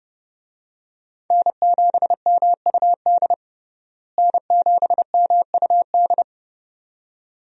Morse code, heard as a keyed tone, about 29 wpm, 700 Hz. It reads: N7MUD N7MUD
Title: USA Call Signs - 5 Characters - Repeated 2x - 30wpm